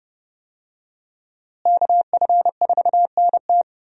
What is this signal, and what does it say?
Morse code, heard as a keyed tone, about 29 wpm, 700 Hz. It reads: KF4NT
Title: USA Call Signs - 5 Characters - Repeated 3x - 30wpm